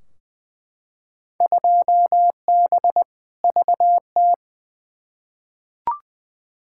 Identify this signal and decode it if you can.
Morse code, heard as a keyed tone, about 20 wpm, 700 Hz. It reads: 2B VT
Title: Field Day Exchanges Encore - 20wpm